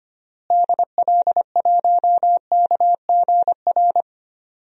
Morse code, heard as a keyed tone, about 25 wpm, 700 Hz. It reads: DL1KGR
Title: DXCC-100 Call Signs - 6 Characters - Repeated 3x - 25wpm